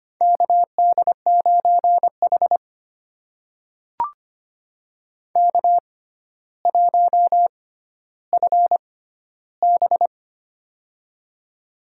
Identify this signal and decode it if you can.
Morse code, heard as a keyed tone, about 25 wpm, spaced as for 13 wpm, 700 Hz. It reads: KD9H K1FB
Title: USA Call Signs - 4 Characters - 6x Character Spacing - 25wpm